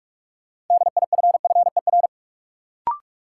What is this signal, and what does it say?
Morse code, heard as a keyed tone, about 46 wpm, 700 Hz. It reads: DIFFER
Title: Sentences from Top 1100 Words - New Word Drill Alt - 45wpm